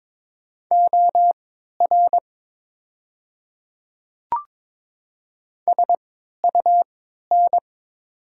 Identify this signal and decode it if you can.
Morse code, heard as a keyed tone, about 22 wpm, spaced as for 12 wpm, 690 Hz. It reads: OR SUN
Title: 3x Character Spacing - Top 300 Words - 22wpm